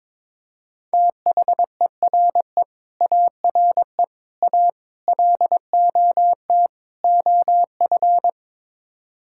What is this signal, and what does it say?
Morse code, heard as a keyed tone, about 22 wpm, 700 Hz. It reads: THERE ARE A LOT OF
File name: practice / 22wpm / Build A Phrase - 5-Word Phrases - 22wpm